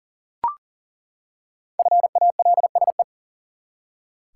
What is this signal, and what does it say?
Morse code, heard as a keyed tone, about 40 wpm, 700 Hz. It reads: FALSE
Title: Five Letter Words - 40wpm